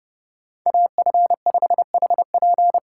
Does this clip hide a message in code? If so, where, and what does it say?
Yes: Morse code from 0.7 to 2.8 s, heard as a keyed tone, about 30 wpm, 700 Hz. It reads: AF5HP